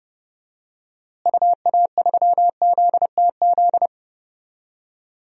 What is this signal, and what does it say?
Morse code, heard as a keyed tone, about 30 wpm, 700 Hz. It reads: UA3ZTZ
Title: DXCC-100 Call Signs - 6 Characters - Repeated 2x - 30wpm